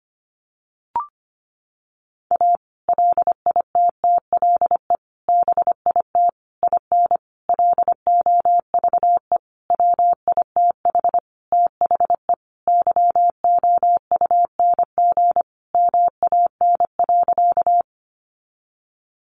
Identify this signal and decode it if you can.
Morse code, heard as a keyed tone, about 25 wpm, 700 Hz. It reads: A LITTLE BIT IN LOVE WITH THE YOUNG MAN.
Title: Sentences from Top 900 Words 25wpm